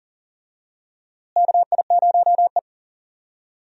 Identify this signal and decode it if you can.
Morse code, heard as a keyed tone, about 40 wpm, 700 Hz. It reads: KI0E
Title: USA Call Signs - 4 Characters - Repeated 3x - 40wpm